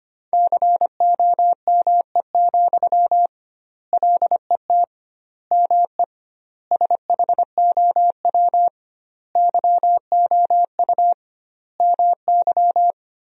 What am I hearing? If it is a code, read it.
Morse code, heard as a keyed tone, about 25 wpm, 700 Hz. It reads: COME, LET ME SHOW YOU MY
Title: 2x Word-Spacing - Sentences from Top 500 Words - 25wpm